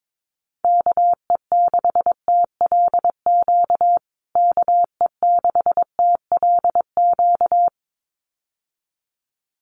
Morse code, heard as a keyed tone, about 22 wpm, 700 Hz. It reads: KE6TLQ KE6TLQ
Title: USA Call Signs Repeated 2x 22wpm